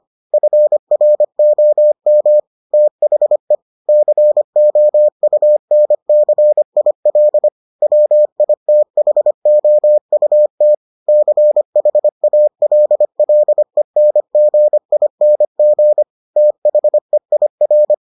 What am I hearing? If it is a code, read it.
Morse code, heard as a keyed tone, about 25 wpm, 600 Hz. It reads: FROM THE COUNCIL WITHOUT CHALLENGING THEIR